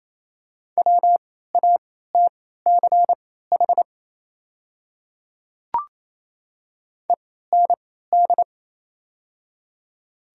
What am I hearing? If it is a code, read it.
Morse code, heard as a keyed tone, about 28 wpm, spaced as for 16 wpm, 700 Hz. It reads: WATCH END
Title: 3x Character Spacing - Top 300 Words - 28wpm